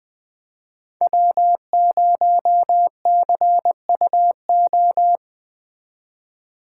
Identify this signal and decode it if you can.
Morse code, heard as a keyed tone, about 21 wpm, 700 Hz. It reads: W0CUO
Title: Speed Racing - USA 1x3 Call Signs 17wpm